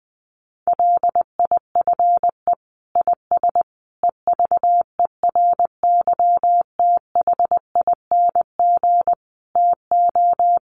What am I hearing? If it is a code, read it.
Morse code, heard as a keyed tone, about 19 wpm, 700 Hz. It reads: LIFE IS EVERYTHING TO
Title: Sentences from Top 700 Words 20wpm